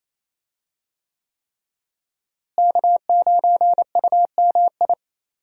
Morse code, heard as a keyed tone, about 28 wpm, 700 Hz. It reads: K9UMI
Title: USA Call Signs - 5 Characters - Repeated 3x - 28wpm